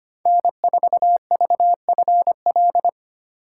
Morse code, heard as a keyed tone, about 25 wpm, 700 Hz. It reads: N4VFL
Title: USA Call Signs - 5 Characters - 3x Character Spacing - 25wpm